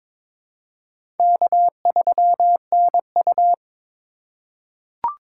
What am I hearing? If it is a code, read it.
Morse code, heard as a keyed tone, about 22 wpm, 700 Hz. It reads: K3NU
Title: USA Call Signs - 8x Character Spacing - 22wpm